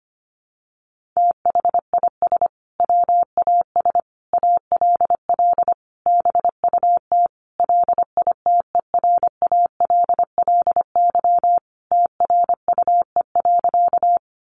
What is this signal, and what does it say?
Morse code, heard as a keyed tone, about 25 wpm, 700 Hz. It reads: THIS WAS ALL BUT LITERALLY TRUE.
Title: Sentences from Top 1400 Words 25wpm